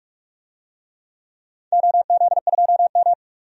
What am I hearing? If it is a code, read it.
Morse code, heard as a keyed tone, about 45 wpm, 700 Hz. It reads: OZ2M
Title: DXCC-100 Call Signs - 4 Characters - 45wpm